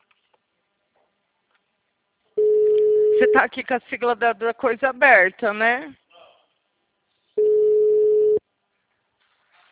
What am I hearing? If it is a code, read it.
Ringback tone (Europe, 425 Hz), with speech.